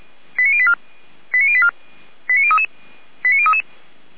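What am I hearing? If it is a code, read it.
Five-tone selective call (ZVEI), heard as quick runs of five tones, tones of about 70 ms. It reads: 89084 89084 89033 89033
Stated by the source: ZVEI five-tone selective call received by SDR.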